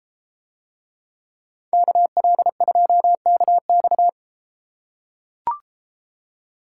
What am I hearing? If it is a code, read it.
Morse code, heard as a keyed tone, about 32 wpm, 700 Hz. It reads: KL2KX